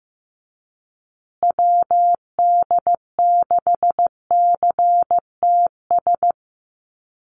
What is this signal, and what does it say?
Morse code, heard as a keyed tone, about 15 wpm, 700 Hz. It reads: WD6CTS
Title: USA Call Signs Repeated 3x 15wpm